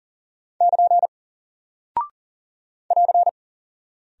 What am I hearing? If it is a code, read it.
Morse code, heard as a keyed tone, about 40 wpm, 700 Hz. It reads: <KN> <AR>
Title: Single Character (Letter:Number:Punctuation:Prosign) Alt 40wpm